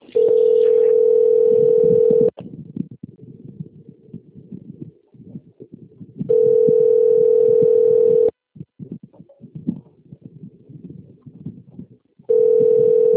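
Ringback tone (North America), 440 + 480 Hz.
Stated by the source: Outgoing telephone call recorded on a phone system labelled Campinas - SP.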